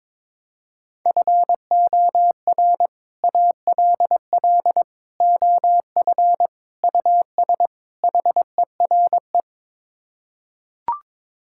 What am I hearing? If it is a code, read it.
Morse code, heard as a keyed tone, about 22 wpm, 700 Hz. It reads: FOR ALL OF US HERE
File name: practice / 22wpm / Build A Phrase - 5-Word Phrases - 22wpm